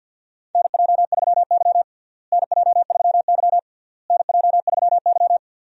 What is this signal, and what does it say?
Morse code, heard as a keyed tone, about 51 wpm, 700 Hz. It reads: NJ3Y NJ3Y NJ3Y
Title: USA Call Signs - 4 Characters - Repeated 3x - 50wpm